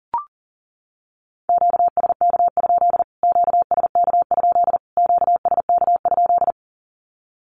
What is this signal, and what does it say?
Morse code, heard as a keyed tone, about 37 wpm, 700 Hz. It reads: QSK? QSK? QSK?
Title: Q-Codes Repeated 3x 40wpm